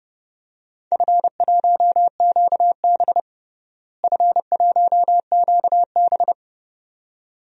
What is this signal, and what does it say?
Morse code, heard as a keyed tone, about 30 wpm, 700 Hz. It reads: F1QB F1QB
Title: DXCC-100 Call Signs - 4 Characters - Repeated 2x - 30wpm